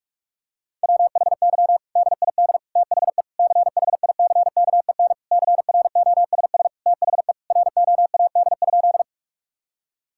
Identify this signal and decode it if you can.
Morse code, heard as a keyed tone, about 46 wpm, 700 Hz. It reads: WHY DID THE CHICKEN CROSS THE ROAD?
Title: Sentences from Top 1000 Words - Review Part 5 - 45wpm